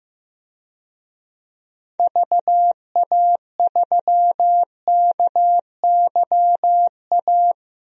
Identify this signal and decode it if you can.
Morse code, heard as a keyed tone, about 15 wpm, 700 Hz. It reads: VA3KYA
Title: DXCC-100 Call Signs - 6 Characters - 15wpm